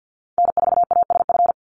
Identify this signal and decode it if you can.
Morse code, heard as a keyed tone, about 50 wpm, 700 Hz. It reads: N4ASF
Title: Call Signs - 50wpm